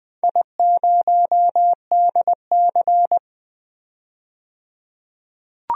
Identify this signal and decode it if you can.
Morse code, heard as a keyed tone, about 20 wpm, 700 Hz. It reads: I0DC
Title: DXCC-100 Call Signs - 4 Characters - 20wpm